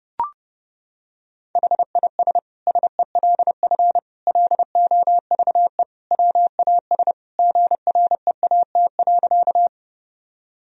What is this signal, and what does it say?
Morse code, heard as a keyed tone, about 30 wpm, 700 Hz. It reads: HIS SELF LOVE WAS GREAT.
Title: Sentences from Top 1000 Words - Review Part 1 - 30wpm